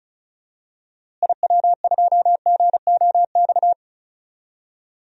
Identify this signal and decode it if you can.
Morse code, heard as a keyed tone, about 34 wpm, 700 Hz. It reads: IW2GOX
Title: DXCC-100 Call Signs - 6 Characters - Repeated 2x - 35wpm